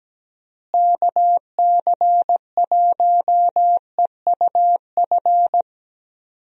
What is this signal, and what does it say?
Morse code, heard as a keyed tone, about 17 wpm, 700 Hz. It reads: KC1EUF
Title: USA Call Signs - 6 Characters - Repeated 2x - 17wpm